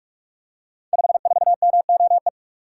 Morse code, heard as a keyed tone, about 44 wpm, 700 Hz. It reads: 54MOE
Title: Code Groups - 5 Letter:Numbers - 45wpm